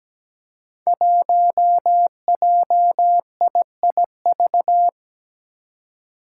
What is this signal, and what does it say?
Morse code, heard as a keyed tone, about 17 wpm, 700 Hz. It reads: 1JIIV
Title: Five Challenging Letter-Number Combinations 17wpm